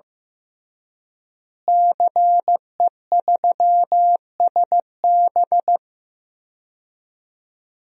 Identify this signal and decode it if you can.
Morse code, heard as a keyed tone, about 15 wpm, 700 Hz. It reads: CE3SB
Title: DXCC-100 Call Signs - 5 Characters - Repeated 2x - 15wpm